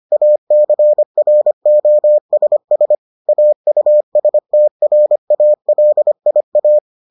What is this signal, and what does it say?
Morse code, heard as a keyed tone, about 25 wpm, 600 Hz. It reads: ACROSS AUSTRALIA